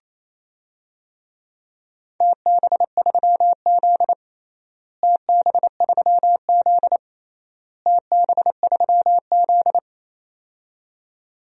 Morse code, heard as a keyed tone, about 27 wpm, 700 Hz. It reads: TB3Z TB3Z TB3Z
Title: DXCC-100 Call Signs - 4 Characters - Repeated 3x - 28wpm